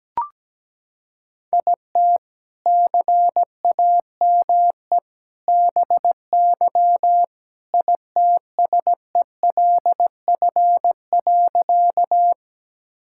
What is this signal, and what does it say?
Morse code, heard as a keyed tone, about 17 wpm, 700 Hz. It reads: IT CAME BY ITSELF.
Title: Sentences from Top 1000 Words - Review Part 5 - 17wpm